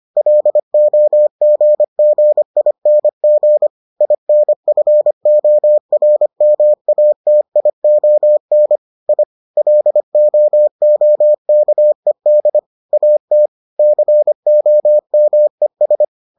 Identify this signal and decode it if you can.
Morse code, heard as a keyed tone, about 25 wpm, 600 Hz. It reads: LOGGING INFORMATION I LOOKED AT COMES